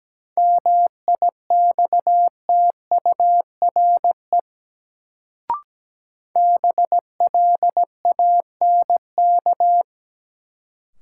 Morse code, heard as a keyed tone, about 17 wpm, 700 Hz. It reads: MIXTURE BLANK
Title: Sentences from Top 1500 Words - New Word Drill - 17wpm